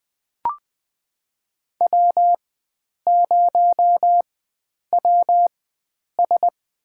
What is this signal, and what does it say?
Morse code, heard as a keyed tone, about 21 wpm, spaced as for 9 wpm, 700 Hz. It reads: W0WS
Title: USA Call Signs - 4x Character Spacing - 20wpm